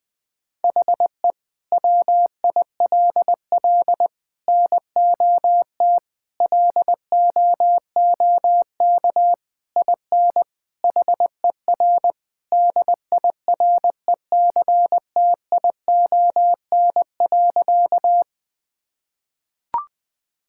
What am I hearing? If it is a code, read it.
Morse code, heard as a keyed tone, about 20 wpm, 700 Hz. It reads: HE WILL NOT LOOK IN HER DIRECTION.
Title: Sentences from Top 1000 Words - Review Part 5 - 20wpm